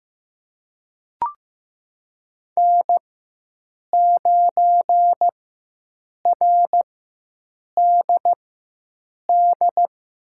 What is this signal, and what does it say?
Morse code, heard as a keyed tone, about 15 wpm, spaced as for 7 wpm, 700 Hz. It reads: N9RDD
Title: USA Call Signs - 5 Characters - 4x Character Spacing - 15wpm